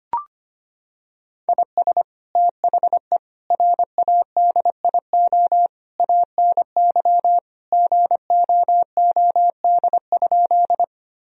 Morse code, heard as a keyed tone, about 25 wpm, 700 Hz. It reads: IS THE RADIO ANY GOOD?
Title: Sentences from Top 1000 Words - Review Part 3 - 25wpm